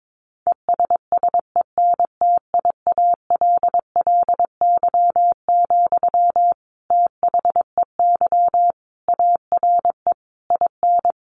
Morse code, heard as a keyed tone, about 22 wpm, 700 Hz. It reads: ESSENTIALLY, THEY ARE IN
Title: Sentences from Top 1300 Words 22wpm